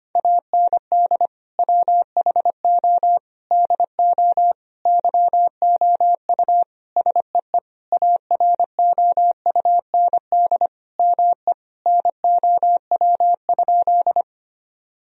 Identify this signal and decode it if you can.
Morse code, heard as a keyed tone, about 25 wpm, 700 Hz. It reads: AND WHO DO YOU SEE AROUND ME NOW?
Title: Sentences from Top 1000 Words - Review Part 3 - 25wpm